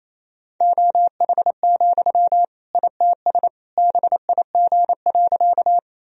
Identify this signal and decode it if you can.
Morse code, heard as a keyed tone, about 27 wpm, 700 Hz. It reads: OH, ITS BIG.
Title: Sentences from Top 1000 Words - Review Part 1 - 28wpm